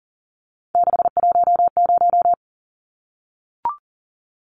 Morse code, heard as a keyed tone, about 40 wpm, 700 Hz. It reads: 610
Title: Numbers - 3-Digit 40wpm